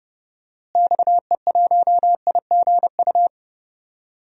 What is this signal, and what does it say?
Morse code, heard as a keyed tone, about 30 wpm, 700 Hz. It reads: XE1IGU
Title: DXCC-100 Call Signs - 6 Characters - Repeated 3x - 30wpm